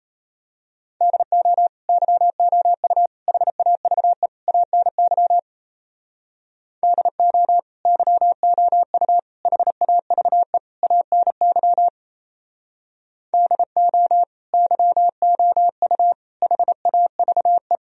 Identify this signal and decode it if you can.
Morse code, heard as a keyed tone, about 32 wpm, 700 Hz. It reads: DO YOU HAVE ANY DO YOU HAVE ANY DO YOU HAVE
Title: Speed-Racing - Sets of 4 Words 28wpm